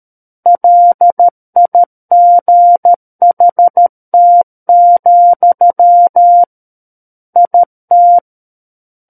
Morse code, heard as a keyed tone, about 13 wpm, 700 Hz. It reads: LIGHT, IT
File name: Book One / 04-Chapter 4 - 13 WPM